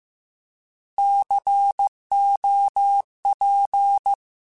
Morse code, heard as a keyed tone, about 15 wpm, 800 Hz. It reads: COP
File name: Alphabetical / Alphabetical mp3s / Cs @13